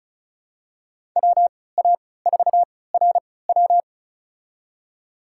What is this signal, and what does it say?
Morse code, heard as a keyed tone, about 36 wpm, spaced as for 20 wpm, 700 Hz. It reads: WA4RW
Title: USA Call Signs - 5 Characters - 3x Character Spacing - 35wpm